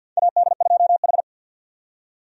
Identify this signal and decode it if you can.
Morse code, heard as a keyed tone, about 48 wpm, 700 Hz. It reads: AD2H
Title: USA Call Signs - 4 Characters - 50wpm